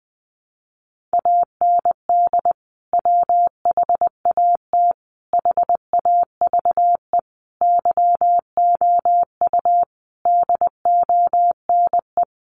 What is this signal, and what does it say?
Morse code, heard as a keyed tone, about 19 wpm, 700 Hz. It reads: AND WHAT HAVE YOU DONE
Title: Sentences from Top 500 Words 20wpm